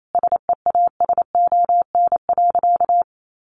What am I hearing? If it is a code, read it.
Morse code, heard as a keyed tone, about 28 wpm, 700 Hz. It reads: SEASON.
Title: Sentences from Top 1500 Words 28wpm